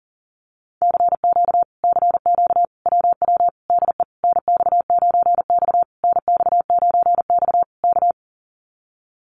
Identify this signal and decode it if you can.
Morse code, heard as a keyed tone, about 40 wpm, 700 Hz. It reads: CQ CQ WW DE NX9X NX9X K
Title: Common QSO Phrases Encore 40wpm